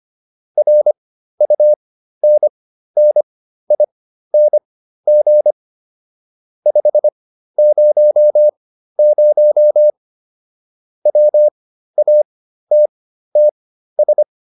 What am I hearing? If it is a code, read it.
Morse code, heard as a keyed tone, about 25 wpm, spaced as for 13 wpm, 600 Hz. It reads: RUNNING 500 WATTS
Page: intermediate-practice-files